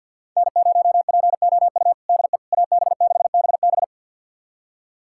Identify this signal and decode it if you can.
Morse code, heard as a keyed tone, about 52 wpm, 700 Hz. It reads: N0POU DE AB6BB
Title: USA Callsign DE Callsign - 5 Characters - 50wpm